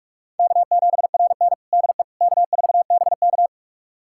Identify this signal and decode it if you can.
Morse code, heard as a keyed tone, about 46 wpm, 700 Hz. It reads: K7RN DE K4BK
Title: USA Callsign DE Callsign - 4 Characters - 45wpm